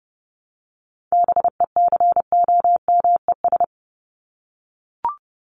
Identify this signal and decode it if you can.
Morse code, heard as a keyed tone, about 30 wpm, 700 Hz. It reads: BECOMES
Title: ES:S Suffix - Top 100 Words with Base Words 30wpm